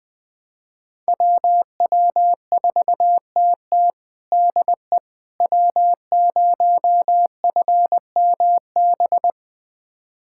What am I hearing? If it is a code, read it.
Morse code, heard as a keyed tone, about 21 wpm, 700 Hz. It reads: WW4TT DE W0FMB
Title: USA Callsign DE Callsign - 5 Characters - 20wpm